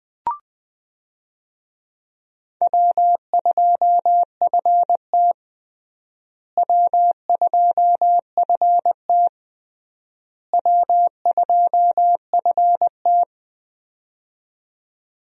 Morse code, heard as a keyed tone, about 20 wpm, 700 Hz. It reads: W2FT W2FT W2FT
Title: USA Call Signs - 4 Characters - Repeated 3x - 20wpm